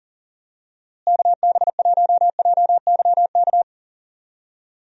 Morse code, heard as a keyed tone, about 39 wpm, 700 Hz. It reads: KB1JYK
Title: USA Call Signs - 6 Characters - Repeated 2x - 40wpm